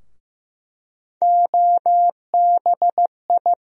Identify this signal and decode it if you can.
Morse code, heard as a keyed tone, about 15 wpm, 700 Hz. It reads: OBI
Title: USA Call Sign Postfixes - 3 Letters 15wpm